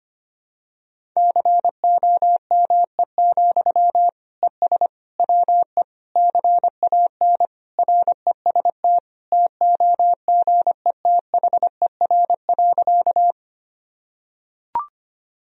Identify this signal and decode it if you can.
Morse code, heard as a keyed tone, about 25 wpm, 700 Hz. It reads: COME, ES WE CAN REST TOGETHER.
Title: Sentences from Top 500 Words and Abbreviations 25wpm